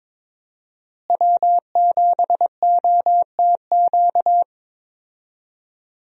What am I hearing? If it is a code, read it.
Morse code, heard as a keyed tone, about 22 wpm, 700 Hz. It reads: W7OTQ
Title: USA Call Signs - 5 Characters - Repeated 2x - 22wpm